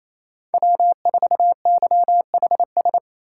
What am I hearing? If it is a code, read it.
Morse code, heard as a keyed tone, about 28 wpm, 700 Hz. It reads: W4YHS
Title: USA Call Signs - 2x Character Spacing - 28wpm